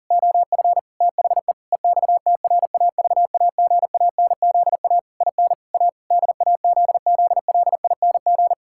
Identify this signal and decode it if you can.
Morse code, heard as a keyed tone, about 40 wpm, 700 Hz. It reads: OF THE EXTRAVAGANZA IN A DAZZLING
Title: Pangram Challenge v2 - 40wpm